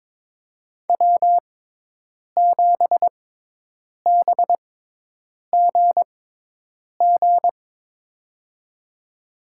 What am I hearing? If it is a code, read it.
Morse code, heard as a keyed tone, about 22 wpm, spaced as for 8 wpm, 700 Hz. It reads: W7BGG
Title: USA Call Signs - 5 Characters - 6x Character Spacing - 22wpm